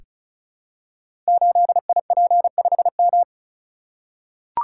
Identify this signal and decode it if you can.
Morse code, heard as a keyed tone, about 36 wpm, 700 Hz. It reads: 8IP5M
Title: Code Groups - 5 Letter:Numbers - 35wpm